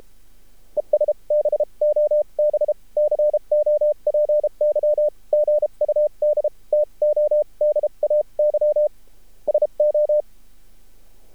Morse code, heard as a keyed tone, about 32 wpm, spaced as for 28 wpm, 600 Hz. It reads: ES BOB COPY GUD TODAY SO
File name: archived / CRSnet